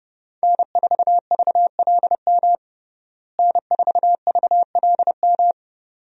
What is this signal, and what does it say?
Morse code, heard as a keyed tone, about 32 wpm, 700 Hz. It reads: N4VLM N4VLM
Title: USA Call Signs - 5 Characters - Repeated 2x - 30wpm